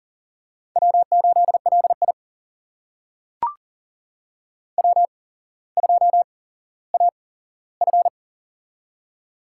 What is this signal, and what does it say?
Morse code, heard as a keyed tone, about 38 wpm, spaced as for 17 wpm, 700 Hz. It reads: W8LI W2AF
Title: USA Call Signs - 4 Characters - 8x Character Spacing - 40wpm